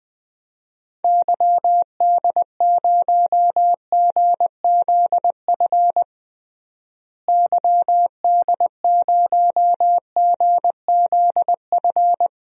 Morse code, heard as a keyed tone, about 20 wpm, 700 Hz. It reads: YD0GZF YD0GZF
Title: DXCC-100 Call Signs - 6 Characters - Repeated 3x - 20wpm